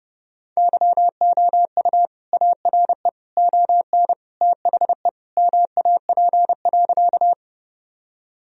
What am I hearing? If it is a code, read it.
Morse code, heard as a keyed tone, about 30 wpm, 700 Hz. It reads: YOU ARE ON THE MAP.
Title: Sentences from Top 1000 Words - Review Part 2 - 30wpm